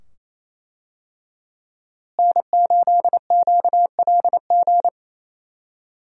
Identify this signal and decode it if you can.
Morse code, heard as a keyed tone, about 27 wpm, 700 Hz. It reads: N8QLG
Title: USA Call Signs - 5 Characters - Repeated 2x - 28wpm